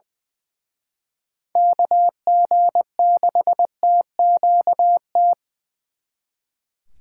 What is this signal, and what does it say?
Morse code, heard as a keyed tone, about 25 wpm, 700 Hz. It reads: KG6TQT
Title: Speed Racing - USA 2x3 Call Signs 20wpm